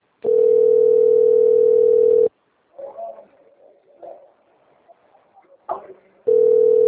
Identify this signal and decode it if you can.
Ringback tone (North America), 440 + 480 Hz, with speech.
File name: rg-1600-unknown- - - .49543